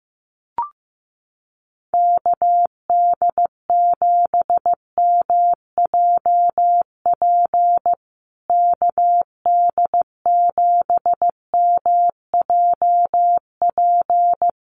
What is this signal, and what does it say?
Morse code, heard as a keyed tone, about 15 wpm, 700 Hz. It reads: KD7MJP KD7MJP
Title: USA Call Signs Repeated 3x 15wpm